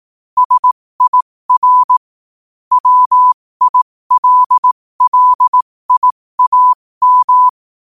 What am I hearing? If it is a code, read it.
Morse code, heard as a keyed tone, about 18 wpm, spaced as for 15 wpm, 1000 Hz. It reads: SIR WILLIAM
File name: qotdfiles / 15wpm